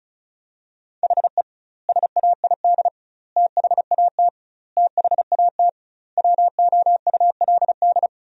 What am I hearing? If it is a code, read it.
Morse code, heard as a keyed tone, about 36 wpm, 700 Hz. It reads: HE SAID THAT THAT WOULD
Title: 2x Word-Spacing - Sentences from Top 200 Words - 35wpm